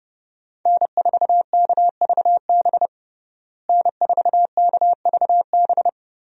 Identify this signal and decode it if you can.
Morse code, heard as a keyed tone, about 31 wpm, 700 Hz. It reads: N4KVB N4KVB
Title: USA Call Signs - 5 Characters - Repeated 3x - 30wpm